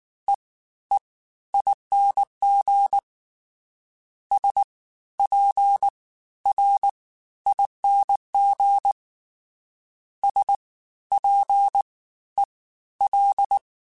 Morse code, heard as a keyed tone, about 19 wpm, spaced as for 13 wpm, 800 Hz. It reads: EEING SPRING SPEL